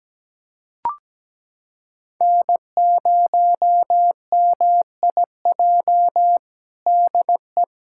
Morse code, heard as a keyed tone, about 17 wpm, 700 Hz. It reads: N0MIJ DE
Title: USA Callsign DE Callsign - 5 Characters - 17wpm